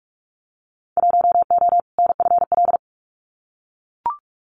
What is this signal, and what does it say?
Morse code, heard as a keyed tone, about 44 wpm, 700 Hz. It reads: 1O NFL
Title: Winter Field Day Exchanges 45wpm